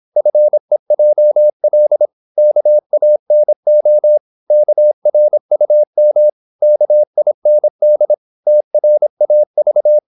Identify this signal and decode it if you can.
Morse code, heard as a keyed tone, about 26 wpm, 600 Hz. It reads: FEJL KANO KRUM KIND TRAV
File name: Korte ord DK 26wpm